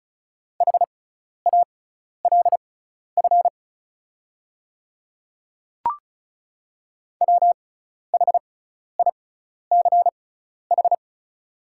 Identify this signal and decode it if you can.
Morse code, heard as a keyed tone, about 36 wpm, spaced as for 12 wpm, 690 Hz. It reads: HALF WHICH
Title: 6x Character Spacing - Top 500 Words - 35wpm